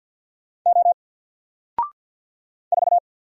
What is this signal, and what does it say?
Morse code, heard as a keyed tone, about 48 wpm, 700 Hz. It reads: O 4
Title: Single Letter-Number Alt 50wpm